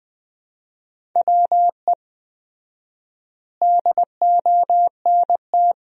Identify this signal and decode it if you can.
Morse code, heard as a keyed tone, about 20 wpm, 700 Hz. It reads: WE DONT
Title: 4x Word-Spacing - Sentences from Top 400 Words - 20wpm